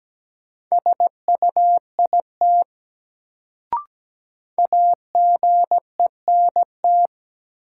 Morse code, heard as a keyed tone, about 17 wpm, 700 Hz. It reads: SUIT AGENT
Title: Sentences from Top 1600 Words - New Word Drill Alt - 17wpm